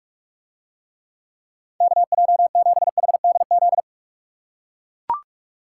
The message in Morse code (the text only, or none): KJ7HDZ